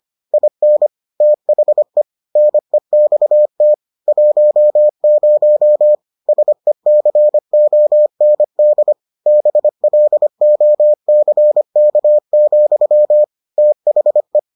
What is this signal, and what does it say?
Morse code, heard as a keyed tone, about 25 wpm, 600 Hz. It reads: IN THE NEXT 10 SECOND BLOCK, THE